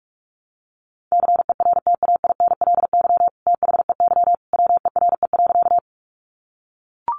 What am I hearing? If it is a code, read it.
Morse code, heard as a keyed tone, about 44 wpm, 700 Hz. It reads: CERTAINLY THEY WERE.
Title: Sentences from Top 900 Words 45wpm